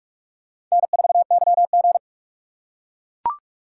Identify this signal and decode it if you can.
Morse code, heard as a keyed tone, about 43 wpm, 700 Hz. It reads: N4YG
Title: USA Call Signs - 4 Characters - 2x Character Spacing - 45wpm